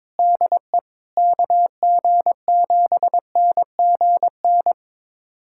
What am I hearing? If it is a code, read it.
Morse code, heard as a keyed tone, about 21 wpm, 700 Hz. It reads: DE KG7NGN
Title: USA Callsign DE Callsign - 6 Characters - 22wpm